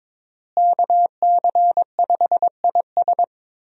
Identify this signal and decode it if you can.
Morse code, heard as a keyed tone, about 21 wpm, 700 Hz. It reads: KC5IS
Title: USA Call Signs - 5 Characters - Repeated 3x - 22wpm